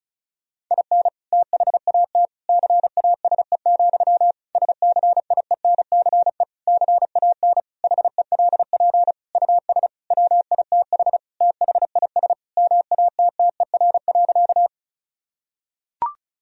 Morse code, heard as a keyed tone, about 36 wpm, 700 Hz. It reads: IN THAT CASE, SCIENCE CAN HELP US WITH THIS MATTER.
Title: Sentences from Top 1000 Words - Review Part 2 - 35wpm